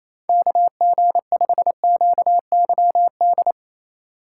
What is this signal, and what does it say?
Morse code, heard as a keyed tone, about 28 wpm, 700 Hz. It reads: KG5QYD
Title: USA Call Signs - 6 Characters - 3x Character Spacing - 28wpm